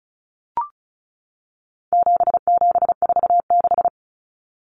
Morse code, heard as a keyed tone, about 34 wpm, 700 Hz. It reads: 7746
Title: Numbers - 4-Digit 35wpm